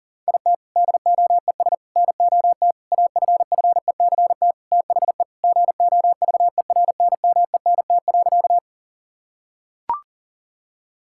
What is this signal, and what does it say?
Morse code, heard as a keyed tone, about 40 wpm, 700 Hz. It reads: IT DOES NOT AFFECT THE GOVERNMENT.
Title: Sentences from Top 1000 Words - Review Part 6 - 40wpm